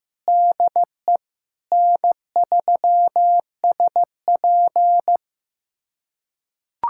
Morse code, heard as a keyed tone, about 15 wpm, 700 Hz. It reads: DE N3SP
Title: USA Callsign DE Callsign - 4 Characters - 15wpm